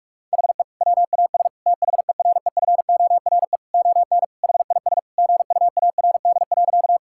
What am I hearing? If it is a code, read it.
Morse code, heard as a keyed tone, about 46 wpm, 700 Hz. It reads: HE WAS THEREFORE ON HIS GUARD.
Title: Sentences from Top 1000 Words - Review Part 5 - 45wpm